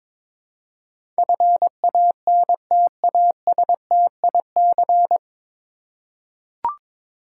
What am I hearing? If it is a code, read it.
Morse code, heard as a keyed tone, about 22 wpm, 700 Hz. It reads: FANTASTIC
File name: 9 Letter Words - First 3 Letters Given - 22wpm